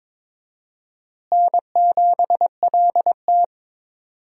Morse code, heard as a keyed tone, about 22 wpm, 700 Hz. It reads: N7LT
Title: USA Call Signs - 4 Characters - Repeated 2x - 22wpm